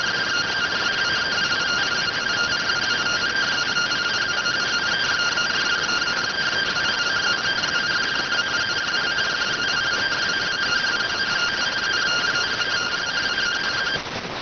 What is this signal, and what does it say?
Two-tone teleprinter signal, about 170 Hz shift.